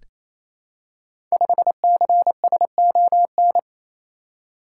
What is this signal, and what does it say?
Morse code, heard as a keyed tone, about 29 wpm, 700 Hz. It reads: 5CSON
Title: Code Groups - 5 Letter:Numbers - 28wpm